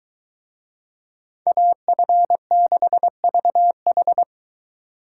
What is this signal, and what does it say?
Morse code, heard as a keyed tone, about 23 wpm, 700 Hz. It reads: AF6VH